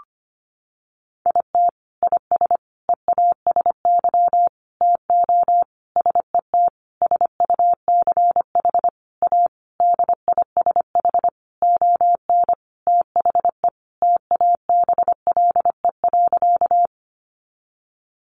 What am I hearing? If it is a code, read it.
Morse code, heard as a keyed tone, about 25 wpm, 700 Hz. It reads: IT IS EASY TO SET SUCH A DISH ON THE TABLE.
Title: Sentences from Top 1300 Words 25wpm